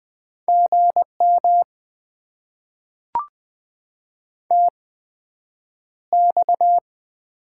Morse code, heard as a keyed tone, about 20 wpm, spaced as for 9 wpm, 700 Hz. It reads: GM TX
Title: Top 90 QSO Elements - 8x Character Spacing - 20wpm